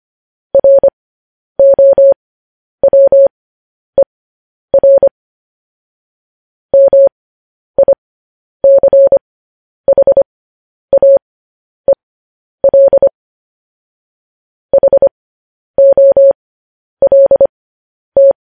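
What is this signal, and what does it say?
Morse code, heard as a keyed tone, about 25 wpm, spaced as for 10 wpm, 550 Hz. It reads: ROWER MICHAEL HOLT
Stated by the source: Morse code news headlines broadcast, opening with vvv.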